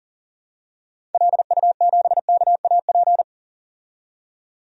Morse code, heard as a keyed tone, about 40 wpm, 700 Hz. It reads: LU7KAP
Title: DXCC-100 Call Signs - 6 Characters - 40wpm